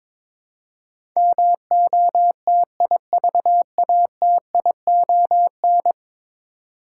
Morse code, heard as a keyed tone, about 22 wpm, 700 Hz. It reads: MOTIVATION
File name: practice / 22wpm / 10 Letter Words - First 3 Letters Given - 22wpm